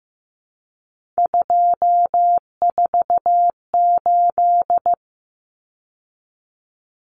Morse code, heard as a keyed tone, about 15 wpm, 700 Hz. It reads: 248
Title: Numbers - 3-Digit 15wpm